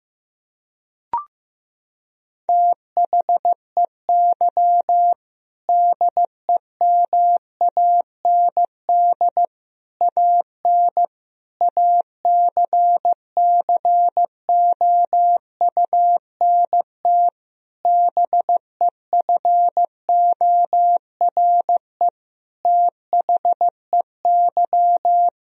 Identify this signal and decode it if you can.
Morse code, heard as a keyed tone, about 16 wpm, 700 Hz. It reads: THEY DEMAND AN ACCOUNT BEFORE THEY
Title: Sentences from Top 1000 Words - Review Part 4 - 15wpm